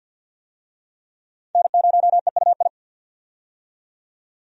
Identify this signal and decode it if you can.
Morse code, heard as a keyed tone, about 50 wpm, 700 Hz. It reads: N0EUI
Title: USA Call Signs - 5 Characters - 50wpm